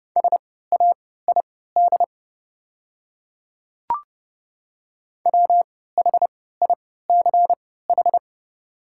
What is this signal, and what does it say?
Morse code, heard as a keyed tone, about 30 wpm, spaced as for 17 wpm, 700 Hz. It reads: SAID WHICH
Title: 3x Character Spacing - Top 100 Words - 30wpm